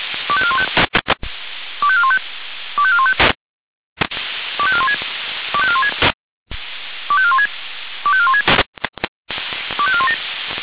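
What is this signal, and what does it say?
Five-tone selective call (ZVEI), heard as quick runs of five tones, tones of about 70 ms. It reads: 25415 25415 25415 25416 25416 25416 25416 25417
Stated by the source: ZVEI five-tone selective call received with an SDR.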